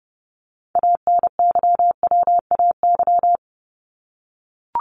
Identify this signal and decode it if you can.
Morse code, heard as a keyed tone, about 30 wpm, 700 Hz. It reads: ANYWAY
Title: Top 1501-1750 Words 30wpm